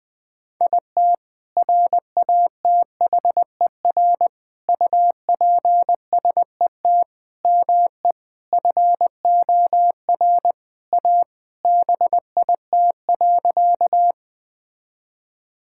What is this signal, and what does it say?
Morse code, heard as a keyed tone, about 20 wpm, 700 Hz. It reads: IT RATHER UPSET ME FOR A BIT.
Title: Sentences from Top 1000 Words - Review Part 5 - 20wpm